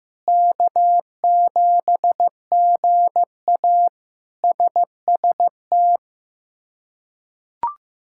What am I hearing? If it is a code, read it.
Morse code, heard as a keyed tone, about 15 wpm, 700 Hz. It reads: K7GA SST
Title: K1USN SST QSO Practice - 15wpm